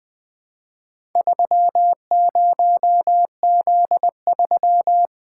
Morse code, heard as a keyed tone, about 21 wpm, 700 Hz. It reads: 30Z3
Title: Four Letter-Number Combinations - 20wpm